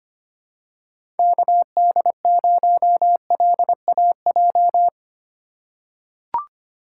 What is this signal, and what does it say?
Morse code, heard as a keyed tone, about 25 wpm, 690 Hz. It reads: KD0LAJ
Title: USA Call Signs - 4x Character Spacing - 25wpm